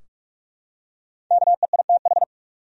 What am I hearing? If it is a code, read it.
Morse code, heard as a keyed tone, about 47 wpm, 700 Hz. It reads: KEITH
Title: USA Common Names - Max 5 Letters - 45wpm